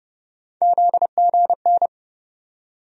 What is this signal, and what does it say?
Morse code, heard as a keyed tone, about 31 wpm, 700 Hz. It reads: ZGN
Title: USA Call Sign Postfixes - 3 Letters 30wpm